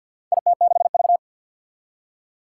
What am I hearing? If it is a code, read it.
Morse code, heard as a keyed tone, about 53 wpm, 700 Hz. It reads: IT6V
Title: Four Challenging Letter-Number Combinations 50wpm